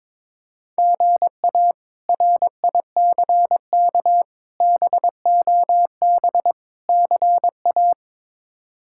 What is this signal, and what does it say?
Morse code, heard as a keyed tone, about 22 wpm, 700 Hz. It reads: GA RICK BOB CA
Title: K1USN SST QSO Practice - 22wpm